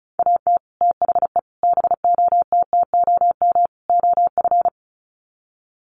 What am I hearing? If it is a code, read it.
Morse code, heard as a keyed tone, about 34 wpm, 700 Hz. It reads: AT THE BOTTOM OF
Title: Sets of 4 Words 35wpm